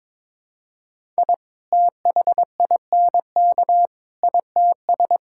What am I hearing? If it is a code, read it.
Morse code, heard as a keyed tone, about 22 wpm, 700 Hz. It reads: I THINK ITS
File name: practice / 22wpm / Sets of 7 Words - 22wpm